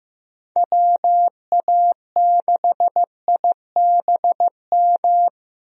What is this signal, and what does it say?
Morse code, heard as a keyed tone, about 15 wpm, 700 Hz. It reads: WA6IBM
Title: USA Call Signs - 6 Characters - Repeated 2x - 15wpm